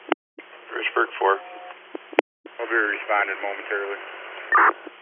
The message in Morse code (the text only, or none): none